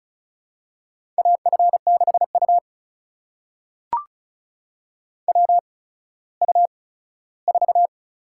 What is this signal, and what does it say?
Morse code, heard as a keyed tone, about 36 wpm, spaced as for 17 wpm, 700 Hz. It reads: AF6U WU4
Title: USA Call Signs - 4 Characters - 8x Character Spacing - 35wpm